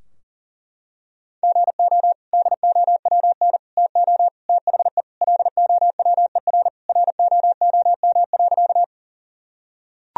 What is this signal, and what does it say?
Morse code, heard as a keyed tone, about 40 wpm, 700 Hz. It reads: GO DOWN TO THE LOWER ROOM.
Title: Sentences from Top 1000 Words - Review Part 4 - 40wpm